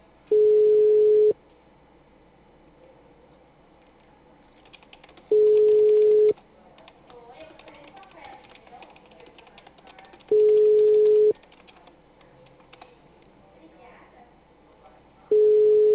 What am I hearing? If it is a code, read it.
Ringback tone (Europe, 425 Hz), with speech.